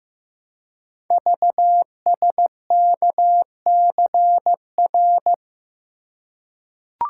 Morse code, heard as a keyed tone, about 15 wpm, 700 Hz. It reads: VSKCR
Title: Code Groups - 5 Letters - 15wpm